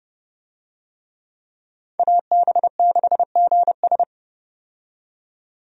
Morse code, heard as a keyed tone, about 30 wpm, 700 Hz. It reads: AB6GS
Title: USA Call Signs - 5 Characters - 30wpm